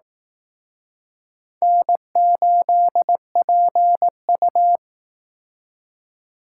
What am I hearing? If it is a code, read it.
Morse code, heard as a keyed tone, about 19 wpm, 700 Hz. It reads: N8PU